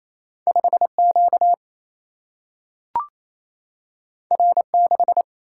Morse code, heard as a keyed tone, about 28 wpm, 700 Hz. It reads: 5Q R6
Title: Single-Letter Single-Number 28wpm